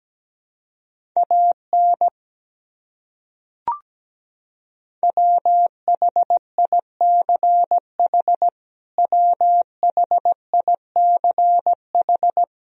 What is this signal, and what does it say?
Morse code, heard as a keyed tone, about 17 wpm, 700 Hz. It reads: AN WHICH WHICH
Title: Top 100 Words Repeated 17wpm